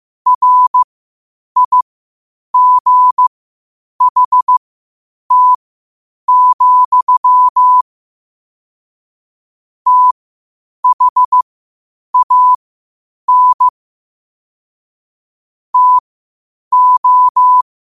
Morse code, heard as a keyed tone, about 15 wpm, spaced as for 8 wpm, 1000 Hz. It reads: RIGHT, THAN TO
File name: qotdfiles / 8wpm